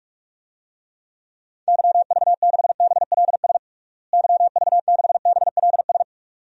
Morse code, heard as a keyed tone, about 41 wpm, 700 Hz. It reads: YV6BLS YV6BLS
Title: DXCC-100 Call Signs - 6 Characters - Repeated 2x - 45wpm